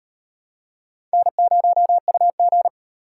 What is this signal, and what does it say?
Morse code, heard as a keyed tone, about 38 wpm, 700 Hz. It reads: N0UG